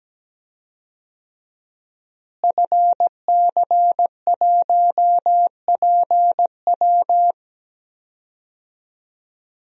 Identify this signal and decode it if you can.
Morse code, heard as a keyed tone, about 17 wpm, 700 Hz. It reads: FC1PW
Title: DXCC-100 Call Signs - 5 Characters - Repeated 3x - 17wpm